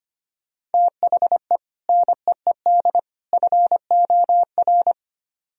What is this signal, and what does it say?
Morse code, heard as a keyed tone, about 33 wpm, 700 Hz. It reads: THE NEED FOR